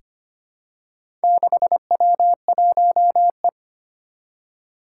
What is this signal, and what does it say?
Morse code, heard as a keyed tone, about 25 wpm, 700 Hz. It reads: 6W1E
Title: Four Challenging Letter-Number Combinations 25wpm